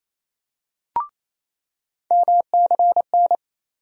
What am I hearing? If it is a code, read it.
Morse code, heard as a keyed tone, about 28 wpm, 700 Hz. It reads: MCN
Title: USA Call Sign Postfixes - 3 Letters 28wpm